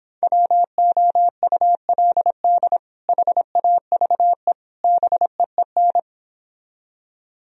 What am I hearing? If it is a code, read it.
Morse code, heard as a keyed tone, about 26 wpm, 700 Hz. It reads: WOULD HAVE BEEN